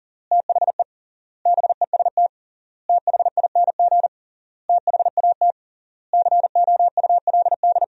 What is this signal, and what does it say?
Morse code, heard as a keyed tone, about 37 wpm, 700 Hz. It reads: THE BEST THING THAT COULD
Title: 3x Word-Spacing - Sentences from Top 400 Words - 40wpm